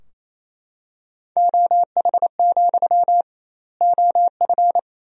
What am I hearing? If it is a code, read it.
Morse code, heard as a keyed tone, about 28 wpm, 700 Hz. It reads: OH, OF
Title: 2x Word-Spacing - Sentences from Top 500 Words - 28wpm